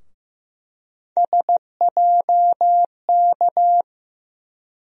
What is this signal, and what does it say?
Morse code, heard as a keyed tone, about 15 wpm, 700 Hz. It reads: SJK
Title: USA Call Sign Postfixes - 3 Letters 15wpm